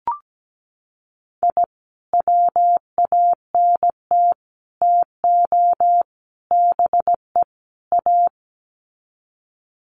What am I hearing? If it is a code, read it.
Morse code, heard as a keyed tone, about 17 wpm, 700 Hz. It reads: I WANT TO BE A
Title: Sets of 5 Words Encore 17wpm